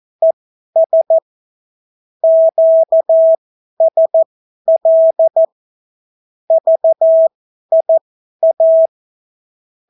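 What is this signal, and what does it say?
Morse code, heard as a keyed tone, about 14 wpm, spaced as for 11 wpm, 650 Hz. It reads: ES QSL VIA
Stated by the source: Example Spanish CW QSO from a Morse learning book, 14 wpm characters with 8 wpm spacing.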